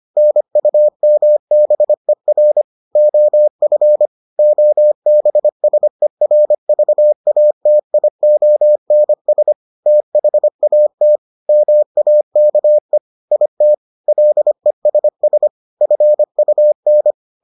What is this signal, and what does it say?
Morse code, heard as a keyed tone, about 25 wpm, 600 Hz. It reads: NUMBER OF OBSERVATIONS THAT MAKE IT LESS FUN